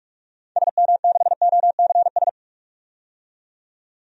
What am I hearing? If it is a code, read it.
Morse code, heard as a keyed tone, about 43 wpm, 700 Hz. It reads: SM6OCS
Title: DXCC-100 Call Signs - 6 Characters - 45wpm